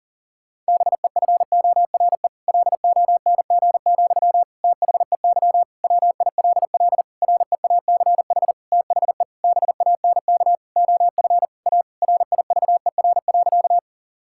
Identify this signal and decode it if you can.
Morse code, heard as a keyed tone, about 40 wpm, 700 Hz. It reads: BEFORE LONG, THEY WILL REACH THE BANK OF A RIVER.
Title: Sentences from Top 1000 Words - Review Part 6 - 40wpm